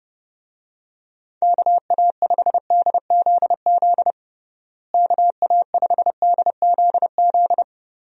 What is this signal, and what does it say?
Morse code, heard as a keyed tone, about 32 wpm, 700 Hz. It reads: KA5DZZ KA5DZZ
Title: USA Call Signs - 6 Characters - Repeated 2x - 30wpm